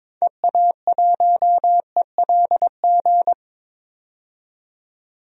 Morse code, heard as a keyed tone, about 21 wpm, 700 Hz. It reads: EA1ELG
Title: DXCC-100 Call Signs - 6 Characters - 22wpm